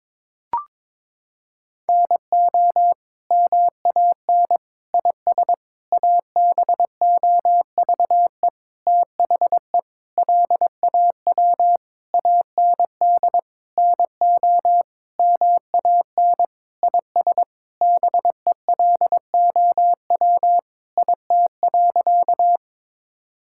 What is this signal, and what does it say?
Morse code, heard as a keyed tone, about 22 wpm, 700 Hz. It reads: NO MAN IS ABOVE THE LAW AND NO MAN IS BELOW IT.
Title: Sentences from Top 1000 Words - Review Part 5 - 22wpm